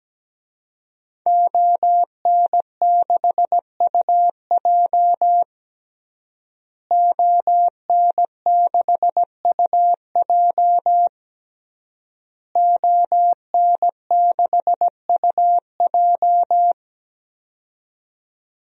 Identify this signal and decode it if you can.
Morse code, heard as a keyed tone, about 17 wpm, 700 Hz. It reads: ON6UJ ON6UJ ON6UJ
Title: DXCC-100 Call Signs - 5 Characters - Repeated 3x - 17wpm